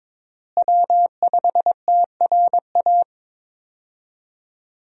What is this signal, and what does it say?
Morse code, heard as a keyed tone, about 21 wpm, 700 Hz. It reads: W5TRA
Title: USA Call Signs - 5 Characters - Repeated 2x - 22wpm